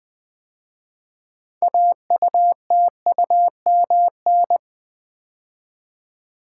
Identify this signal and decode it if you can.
Morse code, heard as a keyed tone, about 20 wpm, 700 Hz. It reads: AUTUMN
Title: 6 Letter Words - First 3 Letters Given - 20wpm